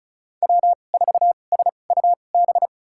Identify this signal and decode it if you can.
Morse code, heard as a keyed tone, about 36 wpm, spaced as for 25 wpm, 700 Hz. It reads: W4SUB
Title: USA Call Signs - 5 Characters - 2x Character Spacing - 35wpm